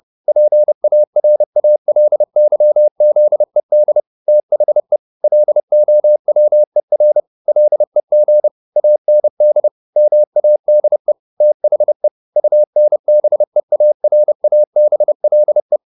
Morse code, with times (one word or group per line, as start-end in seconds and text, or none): 0.3-4.0 PARALYZED
4.3-5.0 THE
5.2-7.2 LOWER
7.5-8.5 LEG
8.8-9.7 AND
10.0-11.1 MADE
11.4-12.1 THE
12.4-15.8 UNBEARABLE